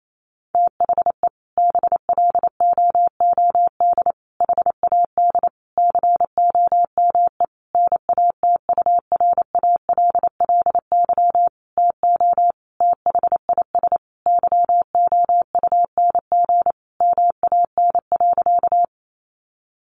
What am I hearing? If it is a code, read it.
Morse code, heard as a keyed tone, about 28 wpm, 700 Hz. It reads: THE BLOOD HAD COME NATURALLY TO THIS YOUNG MAN.
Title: Sentences from Top 1300 Words 28wpm